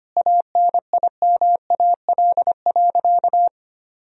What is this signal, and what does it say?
Morse code, heard as a keyed tone, about 25 wpm, 700 Hz. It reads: ANIMAL.
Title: Sentences from Top 1000 Words - Review Part 1 - 25wpm